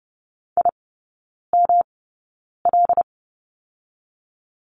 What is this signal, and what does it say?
Morse code, heard as a keyed tone, about 30 wpm, spaced as for 9 wpm, 700 Hz. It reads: IML
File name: Lesson 18e - Letter m - Three Characters 30wpm